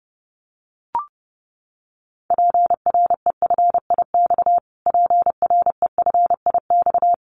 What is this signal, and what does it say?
Morse code, heard as a keyed tone, about 30 wpm, 690 Hz. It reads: PREFIX PREFIX
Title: Callsign Prefixes for Top 25 DXCC - Repeated 3x - 30wpm